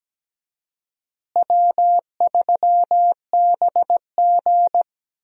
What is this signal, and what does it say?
Morse code, heard as a keyed tone, about 17 wpm, 700 Hz. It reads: W3BG
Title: USA Call Signs - 4 Characters - Repeated 2x - 17wpm